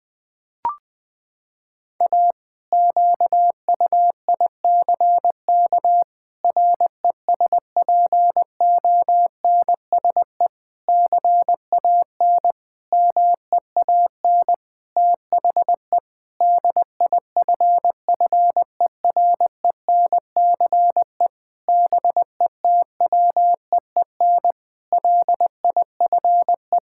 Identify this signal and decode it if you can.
Morse code, heard as a keyed tone, about 21 wpm, 700 Hz. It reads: A QUICK RESPONSE CAN MEAN THE DIFFERENCE BETWEEN LIFE
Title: Sentences with 8-Word Phrases - 20wpm